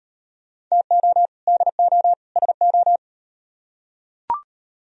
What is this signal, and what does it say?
Morse code, heard as a keyed tone, about 38 wpm, 700 Hz. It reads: TO DO SO